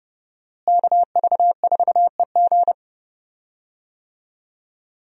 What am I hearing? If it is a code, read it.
Morse code, heard as a keyed tone, about 30 wpm, 700 Hz. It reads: KV4EG
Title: USA Call Signs - 5 Characters - Repeated 3x - 30wpm